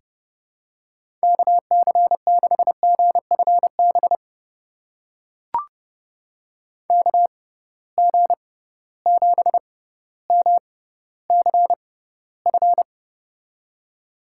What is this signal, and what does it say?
Morse code, heard as a keyed tone, about 30 wpm, spaced as for 15 wpm, 700 Hz. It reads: KC6GFB KG7MCF
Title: USA Call Signs - 6 Characters - 6x Character Spacing - 30wpm